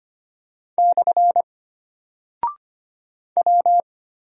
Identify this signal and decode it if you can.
Morse code, heard as a keyed tone, about 25 wpm, 700 Hz. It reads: / W
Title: Single Character (Letter:Number:Punctuation:Prosign) Alt 25wpm